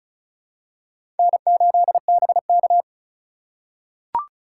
Morse code, heard as a keyed tone, about 36 wpm, 700 Hz. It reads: N8BK
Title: USA Call Signs - 4 Characters - 2x Character Spacing - 35wpm